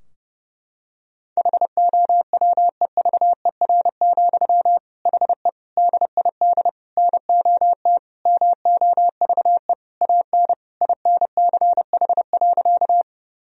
Morse code, heard as a keyed tone, about 30 wpm, 700 Hz. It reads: HOWEVER, HE DID NOT MOVE AN INCH.
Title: Sentences from Top 1000 Words - Review Part 3 - 30wpm